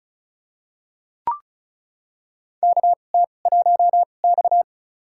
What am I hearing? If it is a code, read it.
Morse code, heard as a keyed tone, about 36 wpm, spaced as for 25 wpm, 700 Hz. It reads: KT1X
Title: USA Call Signs - 4 Characters - 2x Character Spacing - 35wpm